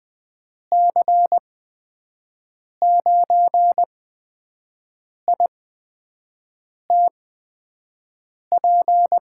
Morse code, heard as a keyed tone, about 20 wpm, spaced as for 5 wpm, 700 Hz. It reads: C9ITP
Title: USA Call Signs - 8x Character Spacing - 20wpm